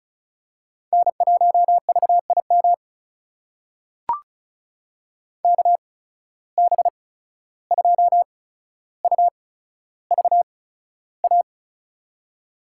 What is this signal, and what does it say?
Morse code, heard as a keyed tone, about 36 wpm, spaced as for 14 wpm, 700 Hz. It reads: N1VIM KB2UVA
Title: USA Call Signs - 8x Character Spacing - 35wpm